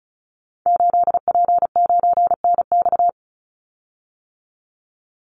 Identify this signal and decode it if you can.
Morse code, heard as a keyed tone, about 34 wpm, 700 Hz. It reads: 8P9NX
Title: Call Signs - 35wpm